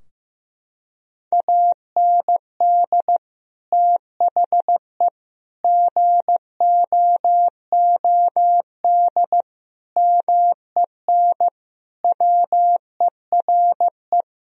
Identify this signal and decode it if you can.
Morse code, heard as a keyed tone, about 15 wpm, 700 Hz. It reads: AND THE GOOD MEN WERE
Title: Sentences from Top 1000 Words - Review Part 2 - 15wpm